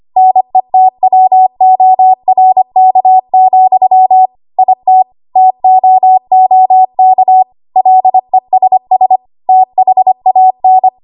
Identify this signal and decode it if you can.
Morse code, heard as a keyed tone, about 25 wpm, 750 Hz. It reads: NETWORK, IT TOOK LESS THAN
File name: Archive / 25 WPM / 25WPM